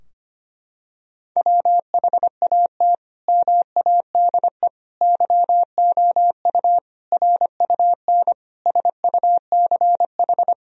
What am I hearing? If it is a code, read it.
Morse code, heard as a keyed tone, about 25 wpm, 700 Hz. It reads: WHAT MADE YOU RUN SUCH
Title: Sentences from Top 1000 Words - Review Part 4 - 25wpm